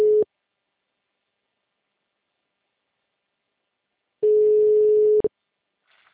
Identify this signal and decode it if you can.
Ringback tone (Europe, 425 Hz).